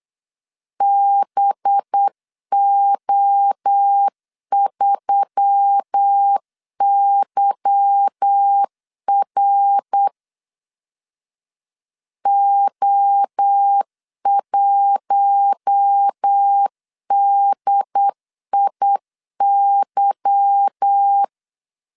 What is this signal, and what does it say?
Morse code, heard as a keyed tone, about 8 wpm, 800 Hz. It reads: BO3YR O1DIY
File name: LEZIONI / 8 YJ45 / 8 Y 8wpm 3245